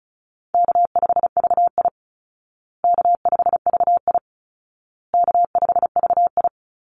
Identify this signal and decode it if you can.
Morse code, heard as a keyed tone, about 34 wpm, 700 Hz. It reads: K5VI K5VI K5VI
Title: Call Signs Repeated 2x - 35wpm